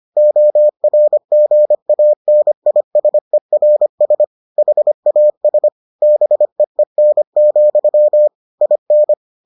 Morse code, heard as a keyed tone, about 25 wpm, 600 Hz. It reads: ORGANISERS HAS BEEN, IN